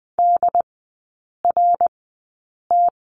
Morse code, heard as a keyed tone, about 20 wpm, spaced as for 8 wpm, 700 Hz. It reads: DRT
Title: 067 Lesson 12e - Letter d - Three Characters 20wpm